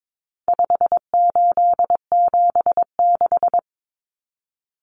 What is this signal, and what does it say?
Morse code, heard as a keyed tone, about 22 wpm, 700 Hz. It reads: 5876
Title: Numbers - 4-Digit 22wpm